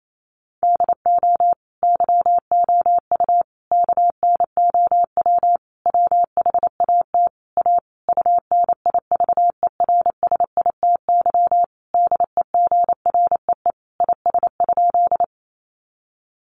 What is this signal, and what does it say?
Morse code, heard as a keyed tone, about 28 wpm, 700 Hz. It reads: DO YOU KNOW WHAT A UNIVERSITY DEGREE IS?
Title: Sentences from Top 1500 Words 28wpm